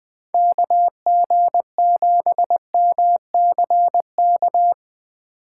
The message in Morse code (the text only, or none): KG7MCK